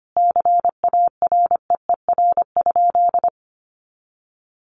Morse code, heard as a keyed tone, about 25 wpm, 700 Hz. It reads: CAREER?
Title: Sentences from Top 1000 Words 25wpm